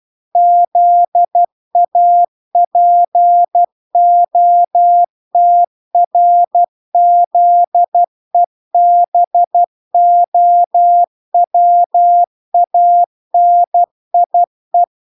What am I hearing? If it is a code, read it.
Morse code, heard as a keyed tone, about 12 wpm, 700 Hz. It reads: ZAPOTRZEBOWANIE